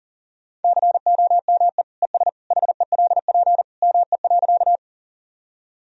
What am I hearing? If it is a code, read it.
Morse code, heard as a keyed tone, about 40 wpm, 700 Hz. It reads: COME ES HELP ME.
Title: Sentences from Top 200 Words and Abbreviations 40wpm